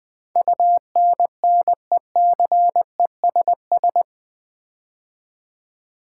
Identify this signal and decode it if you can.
Morse code, heard as a keyed tone, about 20 wpm, 700 Hz. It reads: UNNECESS
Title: Build A Word - 4 Letters At A Time - 11-Letter Words - 20wpm